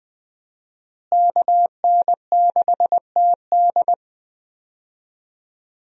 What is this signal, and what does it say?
Morse code, heard as a keyed tone, about 21 wpm, 700 Hz. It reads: KN6TD
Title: USA Call Signs - 5 Characters - Repeated 2x - 20wpm